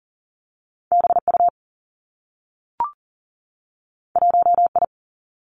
Morse code, heard as a keyed tone, about 39 wpm, 700 Hz. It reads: BU 1I
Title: Two Challenging Letter-Number Combinations 40wpm